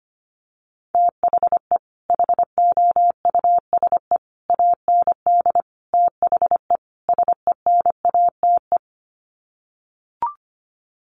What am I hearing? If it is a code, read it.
Morse code, heard as a keyed tone, about 25 wpm, 700 Hz. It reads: THE HOUSE AND THE SENATE
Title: Sets of 5 Words Encore2 25wpm